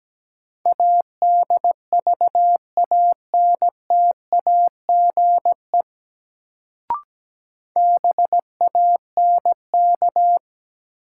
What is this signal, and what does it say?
Morse code, heard as a keyed tone, about 17 wpm, 700 Hz. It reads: ADVANTAGE BANK
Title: Sentences from Top 900 Words - New Word Drill - 17wpm